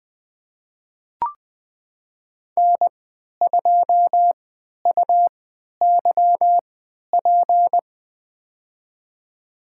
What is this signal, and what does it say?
Morse code, heard as a keyed tone, about 20 wpm, spaced as for 11 wpm, 700 Hz. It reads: N2UYP
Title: USA Call Signs - 5 Characters - 3x Character Spacing - 20wpm